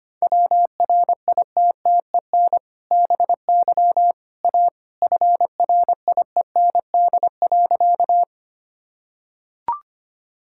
Morse code, heard as a keyed tone, about 25 wpm, 700 Hz. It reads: WRITTEN BY A FRIEND.
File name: practice / 25wpm / Sentences with 2-Word Phrases - 25wpm